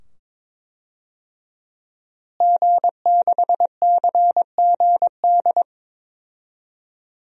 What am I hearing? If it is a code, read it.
Morse code, heard as a keyed tone, about 22 wpm, 700 Hz. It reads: G6CGD
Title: DXCC-100 Call Signs - 5 Characters - Repeated 2x - 22wpm